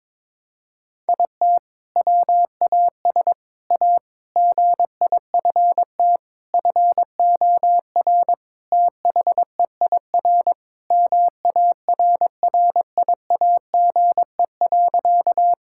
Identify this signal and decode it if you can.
Morse code, heard as a keyed tone, about 22 wpm, 700 Hz. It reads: IT WAS A GIFT FOR THEIR MARRIAGE.
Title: Sentences from Top 1000 Words - Review Part 6 - 22wpm